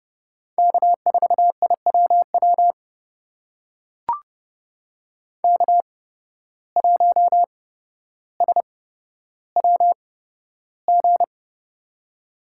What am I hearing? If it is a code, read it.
Morse code, heard as a keyed tone, about 30 wpm, spaced as for 13 wpm, 700 Hz. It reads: K4IWW K1SWG
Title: USA Call Signs - 5 Characters - 8x Character Spacing - 30wpm